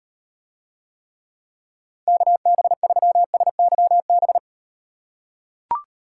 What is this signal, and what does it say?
Morse code, heard as a keyed tone, about 38 wpm, 700 Hz. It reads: KB3SYB